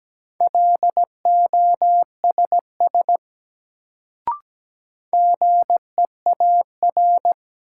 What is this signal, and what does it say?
Morse code, heard as a keyed tone, about 17 wpm, 700 Hz. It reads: LOSS GEAR
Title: Sentences from Top 1400 Words - New Word Drill - 17wpm